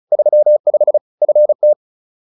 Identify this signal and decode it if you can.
Morse code, heard as a keyed tone, about 34 wpm, 600 Hz. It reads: 35 FT